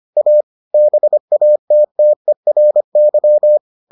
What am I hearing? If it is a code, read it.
Morse code, heard as a keyed tone, about 25 wpm, 600 Hz. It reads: A BATTERY